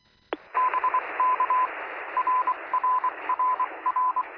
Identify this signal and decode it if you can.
Morse code, heard as a keyed tone, about 23 wpm, 1000 Hz, with radio noise. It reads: BK RRRR
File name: sm7lcb / 0630z yl2aj 23cm